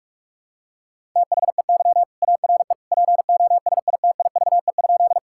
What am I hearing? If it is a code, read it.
Morse code, heard as a keyed tone, about 42 wpm, 690 Hz. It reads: THEY ARE POSITIVE?
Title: Sentences from Top 1000 Words - Review Part 5 - 45wpm